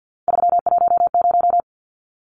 Morse code, heard as a keyed tone, about 50 wpm, 700 Hz. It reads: 310
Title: Numbers - 3-Digit 50wpm